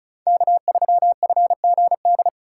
Morse code, heard as a keyed tone, about 36 wpm, 700 Hz. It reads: K3FGD
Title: USA Call Signs - 5 Characters - 3x Character Spacing - 35wpm